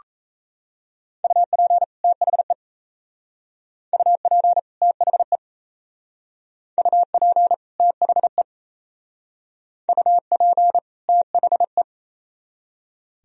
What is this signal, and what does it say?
Morse code, heard as a keyed tone, about 36 wpm, 700 Hz. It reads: UP THE UP THE UP THE UP THE